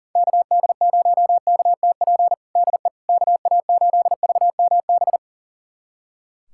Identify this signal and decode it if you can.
Morse code, heard as a keyed tone, about 40 wpm, 700 Hz. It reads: KD0KTP DE KA8VMB
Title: USA Callsign DE Callsign 40wpm